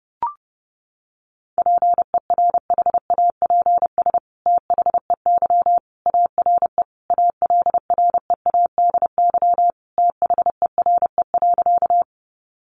Morse code, heard as a keyed tone, about 30 wpm, 700 Hz. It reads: PERHAPS THEY ARE ALREADY THERE.
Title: Sentences from Top 700 Words 30wpm